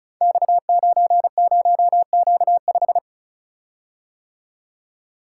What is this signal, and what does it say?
Morse code, heard as a keyed tone, about 36 wpm, 700 Hz. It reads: X90Q5
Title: Code Groups - 5 Letter:Numbers - 35wpm